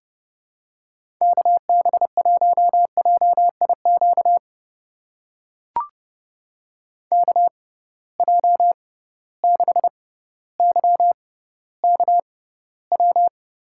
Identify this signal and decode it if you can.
Morse code, heard as a keyed tone, about 30 wpm, spaced as for 15 wpm, 700 Hz. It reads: KB1JIQ KJ6YKW
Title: USA Call Signs - 6 Characters - 6x Character Spacing - 30wpm